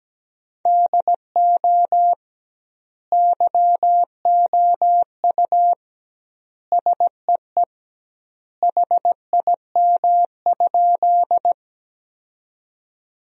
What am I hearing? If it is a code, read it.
Morse code, heard as a keyed tone, about 17 wpm, 700 Hz. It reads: DO YOU SEE HIM?
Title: 2x Word-Spacing - Sentences from Top 100 Words - 17wpm